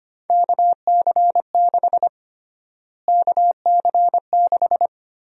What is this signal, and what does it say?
Morse code, heard as a keyed tone, about 25 wpm, 700 Hz. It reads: KC6 KC6
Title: USA Call Sign Prefixes - Repeated 2x - 25wpm